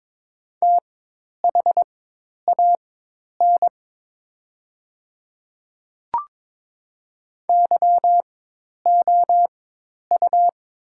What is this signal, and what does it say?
Morse code, heard as a keyed tone, about 22 wpm, spaced as for 10 wpm, 700 Hz. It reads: THAN YOU
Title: 4x Character Spacing - Top 300 Words - 22wpm